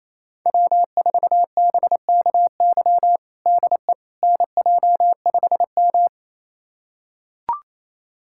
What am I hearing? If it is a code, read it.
Morse code, heard as a keyed tone, about 28 wpm, 700 Hz. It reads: W4BKY DE NJ5M
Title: USA Callsign DE Callsign 28wpm